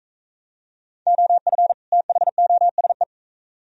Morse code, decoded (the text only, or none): OF THOSE